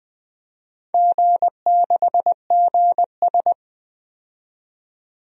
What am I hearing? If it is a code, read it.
Morse code, heard as a keyed tone, about 20 wpm, 700 Hz. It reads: G6GS
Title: DXCC-100 Call Signs - 4 Characters - 20wpm